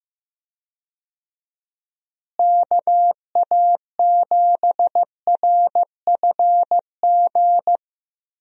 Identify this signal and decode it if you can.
Morse code, heard as a keyed tone, about 15 wpm, 700 Hz. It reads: KA7RFG
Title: USA Call Signs - 6 Characters - Repeated 2x - 15wpm